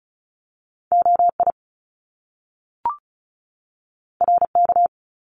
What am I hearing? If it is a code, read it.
Morse code, heard as a keyed tone, about 36 wpm, 700 Hz. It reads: OI RK
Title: Top Two Letter Combinations in Words 35wpm